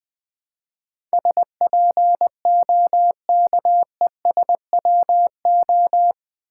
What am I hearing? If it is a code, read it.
Morse code, heard as a keyed tone, about 20 wpm, 700 Hz. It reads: SPOKESWO
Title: Build A Word - 4 Letters At A Time - 11-Letter Words - 20wpm